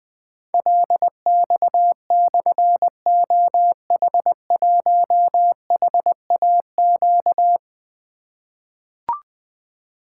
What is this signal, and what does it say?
Morse code, heard as a keyed tone, about 21 wpm, 700 Hz. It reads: LX/OH1HAQ
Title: Difficult Call Signs 20wpm